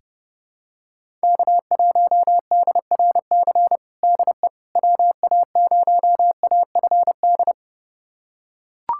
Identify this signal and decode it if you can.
Morse code, heard as a keyed tone, about 30 wpm, 700 Hz. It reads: K1DRC DE WA0AFD
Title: USA Callsign DE Callsign 30wpm